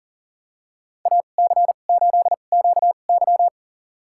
Morse code, heard as a keyed tone, about 40 wpm, spaced as for 29 wpm, 700 Hz. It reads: AC8QY
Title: USA Call Signs - 5 Characters - 2x Character Spacing - 40wpm